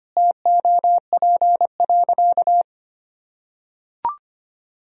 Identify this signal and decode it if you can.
Morse code, heard as a keyed tone, about 25 wpm, 700 Hz. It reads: TOP.
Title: Sentences from Top 1000 Words - Review Part 2 - 25wpm